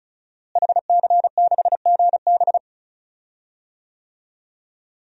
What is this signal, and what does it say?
Morse code, heard as a keyed tone, about 34 wpm, 700 Hz. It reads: HC6GB
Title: DXCC-100 Call Signs - 5 Characters - Repeated 2x - 35wpm